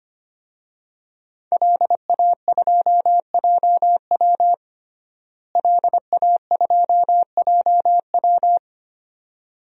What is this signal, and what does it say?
Morse code, heard as a keyed tone, about 25 wpm, 700 Hz. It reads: LA2JW LA2JW
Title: DXCC-100 Call Signs - 5 Characters - Repeated 2x - 25wpm